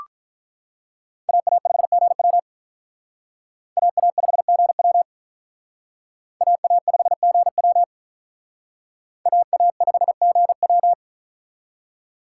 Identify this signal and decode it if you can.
Morse code, heard as a keyed tone, about 44 wpm, 700 Hz. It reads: AA5GW AA5GW AA5GW AA5GW